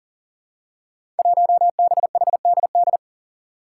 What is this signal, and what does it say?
Morse code, heard as a keyed tone, about 40 wpm, 700 Hz. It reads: 1BHDD
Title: Five Challenging Letter-Number Combinations 40wpm